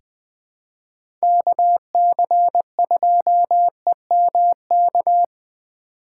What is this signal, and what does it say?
Morse code, heard as a keyed tone, about 20 wpm, 700 Hz. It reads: KC2EMK
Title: USA Call Signs - 6 Characters - 6x Character Spacing - 20wpm